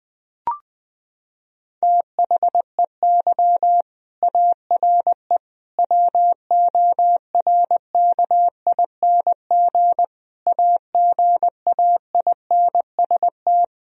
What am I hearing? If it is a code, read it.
Morse code, heard as a keyed tone, about 20 wpm, 700 Hz. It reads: THEY ARE WORKING AGAINST
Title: Sentences from Top 1000 Words - Review Part 4 - 20wpm